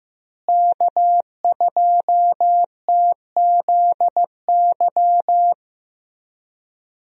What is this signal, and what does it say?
Morse code, heard as a keyed tone, about 15 wpm, 700 Hz. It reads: K2TZY
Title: USA Call Signs - 5 Characters - Repeated 3x - 15wpm